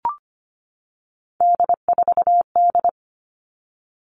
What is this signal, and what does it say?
Morse code, heard as a keyed tone, about 25 wpm, 700 Hz. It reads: D4D
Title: All Cut Numbers - 3-Digit 25wpm